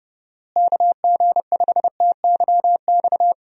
Morse code, heard as a keyed tone, about 32 wpm, 700 Hz. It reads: KG5TYX
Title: USA Call Signs - 6 Characters - Repeated 3x - 30wpm